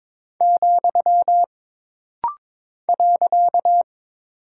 Morse code, heard as a keyed tone, about 22 wpm, 700 Hz. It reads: , .
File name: practice / 22wpm / Single Punctuation-Prosign - Rapid-Fire - 22wpm